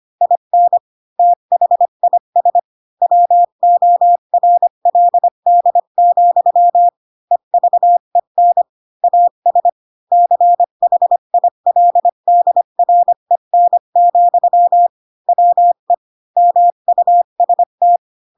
Morse code, heard as a keyed tone, about 25 wpm, 700 Hz. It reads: IN THIS WORLD, EVEN AS CHILDREN, WE MUST
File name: Pinocchio / Pinocchio ch7 20 25 ik1hge